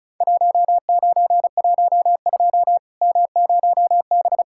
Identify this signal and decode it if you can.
Morse code, heard as a keyed tone, about 35 wpm, 700 Hz. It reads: 1912 M0B
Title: Worked All Europe DX Contest - QTC Exchange - 35wpm